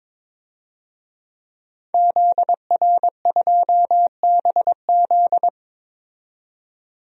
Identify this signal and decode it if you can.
Morse code, heard as a keyed tone, about 22 wpm, 700 Hz. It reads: ZR2BZ
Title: DXCC-100 Call Signs - 5 Characters - 22wpm